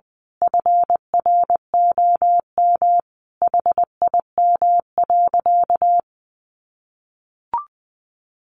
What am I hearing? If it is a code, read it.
Morse code, heard as a keyed tone, about 20 wpm, 700 Hz. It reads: FROM HIM.
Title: Sentences from Top 1500 Words 20wpm